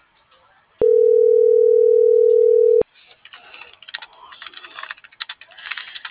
Ringback tone (North America), 440 + 480 Hz.